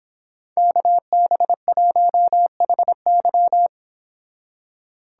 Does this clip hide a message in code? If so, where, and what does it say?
Yes: Morse code from 0.6 to 3.7 s, heard as a keyed tone, about 26 wpm, 700 Hz. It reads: KB1HY